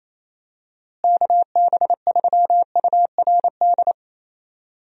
Morse code, heard as a keyed tone, about 28 wpm, 700 Hz. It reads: KB3URD
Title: USA Call Signs - 6 Characters - Repeated 3x - 28wpm